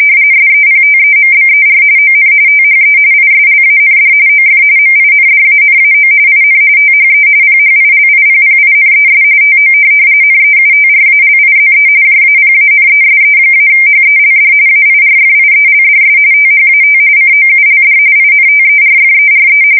RTTY, about 45.45 baud. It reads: SEPTEMBER ON THE ISLAND OF MINDANAO IN THE PHILIPPINES TO NEUTRALIZE JAPANESE-HELD AIRFIELDS THAT COULD INTERFERE